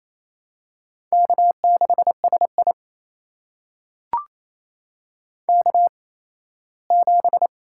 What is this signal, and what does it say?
Morse code, heard as a keyed tone, about 28 wpm, spaced as for 17 wpm, 700 Hz. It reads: K6SI K7
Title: USA Call Signs - 4 Characters - 8x Character Spacing - 28wpm